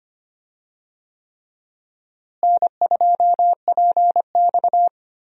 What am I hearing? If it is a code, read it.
Morse code, heard as a keyed tone, about 25 wpm, 700 Hz. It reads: N2PX
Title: USA Call Signs - 4 Characters - Repeated 3x - 25wpm